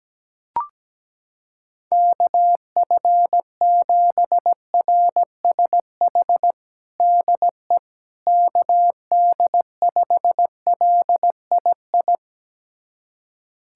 Morse code, heard as a keyed tone, about 17 wpm, 700 Hz. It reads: KF7RSH DE KD5LII
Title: USA Callsign DE Callsign 17wpm